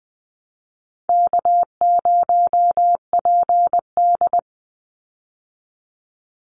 Morse code, heard as a keyed tone, about 20 wpm, 700 Hz. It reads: K0PD
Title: Easy Call Signs Repeated 2x - 20wpm